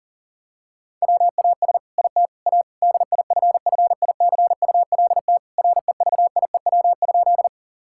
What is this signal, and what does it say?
Morse code, heard as a keyed tone, about 40 wpm, 700 Hz. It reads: WAS IT A DIFFICULT REVIEW?
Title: Sentences from Top 1000 Words - Review Part 6 - 40wpm